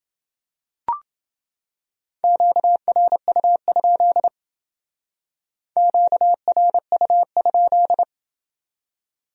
Morse code, heard as a keyed tone, about 28 wpm, 700 Hz. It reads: QRU? QRU?